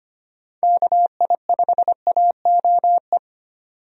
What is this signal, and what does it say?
Morse code, heard as a keyed tone, about 25 wpm, 700 Hz. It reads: KI5AOE
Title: USA Call Signs - 6 Characters - 25wpm